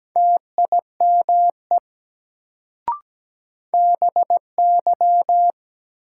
Morse code, heard as a keyed tone, about 17 wpm, 700 Hz. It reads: TIME BY
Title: Sentences from Top 100 Words - New Word Drill - 17wpm